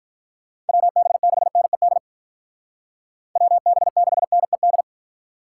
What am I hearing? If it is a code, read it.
Morse code, heard as a keyed tone, about 44 wpm, 700 Hz. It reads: WB6NED WB6NED
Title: Speed Racing - USA 2x3 Call Signs 35wpm